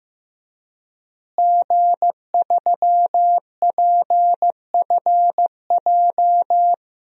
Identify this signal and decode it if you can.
Morse code, heard as a keyed tone, about 15 wpm, 700 Hz. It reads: G3PFJ
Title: DXCC-100 Call Signs - 5 Characters - Repeated 2x - 15wpm